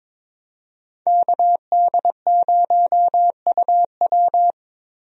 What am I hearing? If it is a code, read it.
Morse code, heard as a keyed tone, about 22 wpm, 700 Hz. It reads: KD0UW
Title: USA Call Signs - 5 Characters - Repeated 2x - 22wpm